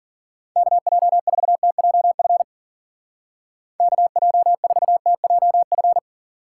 Morse code, heard as a keyed tone, about 51 wpm, 700 Hz. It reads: KJ4TJF KJ4TJF